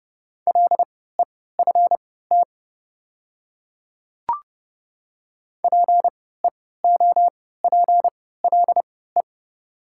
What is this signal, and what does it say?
Morse code, heard as a keyed tone, about 30 wpm, spaced as for 17 wpm, 700 Hz. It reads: LEFT PEOPLE
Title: 3x Character Spacing - Top 300 Words - 30wpm